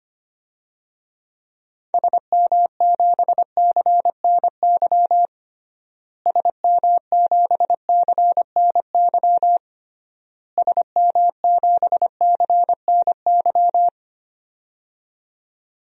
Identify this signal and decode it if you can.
Morse code, heard as a keyed tone, about 25 wpm, 700 Hz. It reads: SM7CNY SM7CNY SM7CNY
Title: DXCC-100 Call Signs - 6 Characters - Repeated 3x - 25wpm